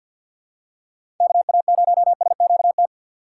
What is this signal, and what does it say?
Morse code, heard as a keyed tone, about 53 wpm, 700 Hz. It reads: KA0SQT
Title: USA Call Signs - 6 Characters - 6x Character Spacing - 50wpm